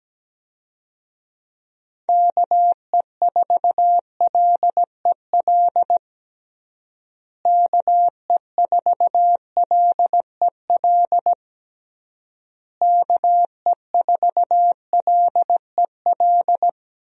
Morse code, heard as a keyed tone, about 17 wpm, 700 Hz. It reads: KE4LEL KE4LEL KE4LEL
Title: USA Call Signs - 6 Characters - Repeated 3x - 17wpm